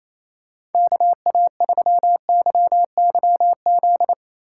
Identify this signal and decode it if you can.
Morse code, heard as a keyed tone, about 28 wpm, 700 Hz. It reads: KA3YYZ
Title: USA Call Signs - 6 Characters - 3x Character Spacing - 28wpm